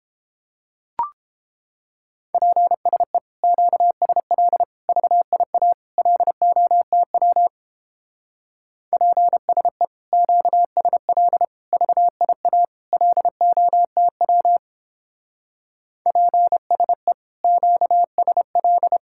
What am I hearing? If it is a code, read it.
Morse code, heard as a keyed tone, about 31 wpm, 700 Hz. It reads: PSE QSL VIA LOTW PSE QSL VIA LOTW PSE QSL